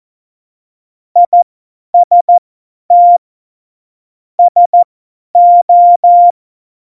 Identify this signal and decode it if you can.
Morse code, heard as a keyed tone, about 14 wpm, spaced as for 10 wpm, 700 Hz. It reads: IST SO
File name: Morsekurs / Witze / Witz 08. September 2020, 50 F70